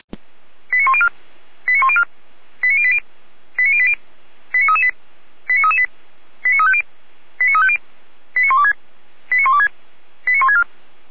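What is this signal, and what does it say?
Five-tone selective call (ZVEI), heard as quick runs of five tones, tones of about 70 ms. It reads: 89104 89104 89080 89080 89338 89338 89350 89350 89126 89126 89164
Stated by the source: Five-tone ZVEI selective call received on SDR.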